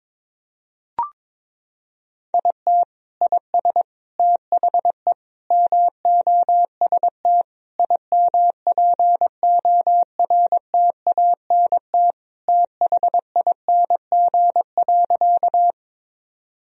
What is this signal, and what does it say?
Morse code, heard as a keyed tone, about 22 wpm, 700 Hz. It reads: IT IS THE MOST IMPORTANT THING.
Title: Sentences from Top 1000 Words - Review Part 1 - 22wpm